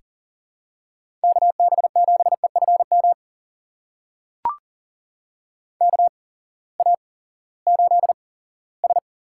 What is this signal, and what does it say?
Morse code, heard as a keyed tone, about 40 wpm, spaced as for 20 wpm, 700 Hz. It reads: KB7EFM KA8S
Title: USA Call Signs - 6 Characters - 8x Character Spacing - 40wpm